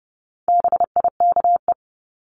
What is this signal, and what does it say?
Morse code, heard as a keyed tone, about 30 wpm, 700 Hz. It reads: BIKE
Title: Top 1751-2000 Words 30wpm